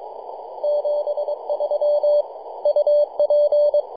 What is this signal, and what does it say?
Morse code, heard as a keyed tone, about 24 wpm, 590 Hz. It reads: 73 UP
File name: 3d2ya-18-085-cw